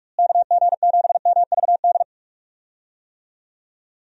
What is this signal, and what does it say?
Morse code, heard as a keyed tone, about 47 wpm, 700 Hz. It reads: KG7MVD
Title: USA Call Signs - 6 Characters - Repeated 2x - 45wpm